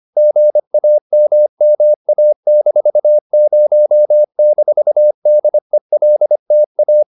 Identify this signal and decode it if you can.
Morse code, heard as a keyed tone, about 25 wpm, 600 Hz. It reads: GAMMA-0-DELTA